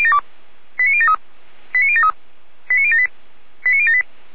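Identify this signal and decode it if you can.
Five-tone selective call (ZVEI), heard as quick runs of five tones, tones of about 70 ms. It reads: 89073 89073 89078 89078
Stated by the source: ZVEI five-tone selective call 89067 received on SDR, logged by Feuerwehr Orlamünde.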